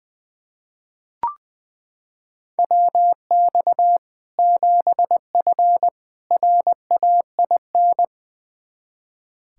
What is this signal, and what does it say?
Morse code, heard as a keyed tone, about 21 wpm, 700 Hz. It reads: WX 7F RAIN
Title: Practice QSO - Exchange 2 - Info - 20wpm